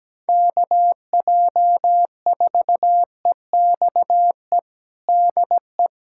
Morse code, heard as a keyed tone, about 17 wpm, 700 Hz. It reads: KJ4EXE DE
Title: USA Callsign DE Callsign - 6 Characters - 17wpm